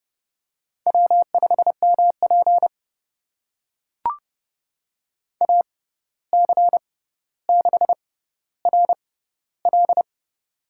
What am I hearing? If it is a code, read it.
Morse code, heard as a keyed tone, about 30 wpm, spaced as for 14 wpm, 700 Hz. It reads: W5MP AC6RL
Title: USA Call Signs - 6x Character Spacing - 30wpm